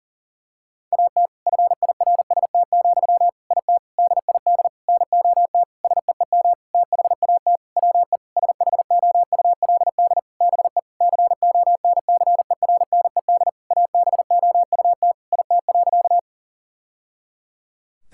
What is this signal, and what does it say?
Morse code, heard as a keyed tone, about 37 wpm, 700 Hz. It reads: AT FIRST, IT DID NOT SEEM THAT WE SHOULD BE CONCERNED ABOUT IT.
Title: Sentences from Top 1000 Words - Review Part 6 - 40wpm